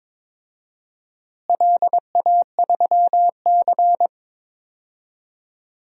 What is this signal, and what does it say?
Morse code, heard as a keyed tone, about 22 wpm, 700 Hz. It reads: LA3C
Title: DXCC-100 Call Signs - 4 Characters - Repeated 3x - 22wpm